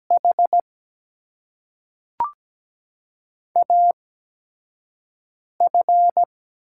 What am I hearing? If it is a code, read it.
Morse code, heard as a keyed tone, about 17 wpm, spaced as for 4 wpm, 700 Hz. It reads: HAF
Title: USA Call Signs - 4 Characters - 8x Character Spacing - 17wpm